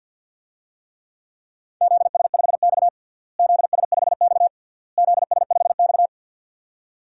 Morse code, heard as a keyed tone, about 51 wpm, 700 Hz. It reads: ZS5X ZS5X ZS5X
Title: DXCC-100 Call Signs - 4 Characters - Repeated 3x - 50wpm